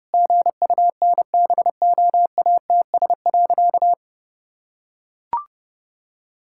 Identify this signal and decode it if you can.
Morse code, heard as a keyed tone, about 30 wpm, 700 Hz. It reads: GUNBOATS.
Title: Pangram Challenge v2 - 30wpm